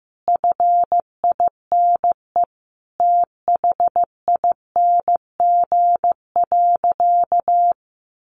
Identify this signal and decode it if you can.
Morse code, heard as a keyed tone, about 14 wpm, 700 Hz. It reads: FINE THING.
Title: Sentences from Top 500 Words 15wpm